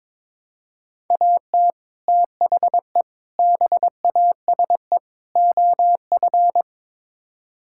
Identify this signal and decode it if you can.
Morse code, heard as a keyed tone, about 22 wpm, 700 Hz. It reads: AT THE BASE OF
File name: practice / 22wpm / Build A Phrase - 5-Word Phrases - 22wpm